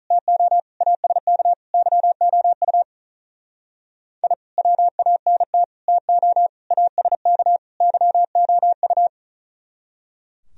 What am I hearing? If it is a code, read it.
Morse code, heard as a keyed tone, about 45 wpm, 700 Hz. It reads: TO ASK YOU I WANT TO ASK YOU